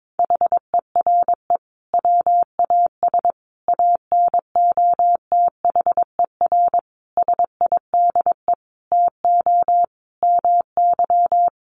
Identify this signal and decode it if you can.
Morse code, heard as a keyed tone, about 22 wpm, 700 Hz. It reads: HERE WAS ANOTHER SIDE TO MY
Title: Sentences from Top 1500 Words 22wpm